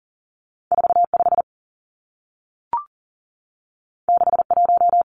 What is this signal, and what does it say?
Morse code, heard as a keyed tone, about 40 wpm, 700 Hz. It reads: 45 61
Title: Numbers - 2-Digit 40wpm